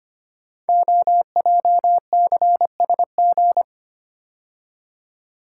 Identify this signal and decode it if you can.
Morse code, heard as a keyed tone, about 25 wpm, 700 Hz. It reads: OJCSG
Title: Code Groups - 5 Letters - 25wpm